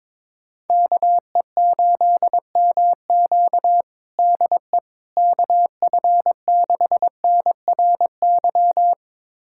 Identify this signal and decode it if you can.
Morse code, heard as a keyed tone, about 22 wpm, 700 Hz. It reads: KE8MQ DE KF6NRY
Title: USA Callsign DE Callsign 22wpm